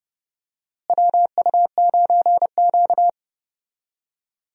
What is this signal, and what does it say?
Morse code, heard as a keyed tone, about 32 wpm, 700 Hz. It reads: WU9Q
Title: USA Call Signs - 4 Characters - Repeated 2x - 30wpm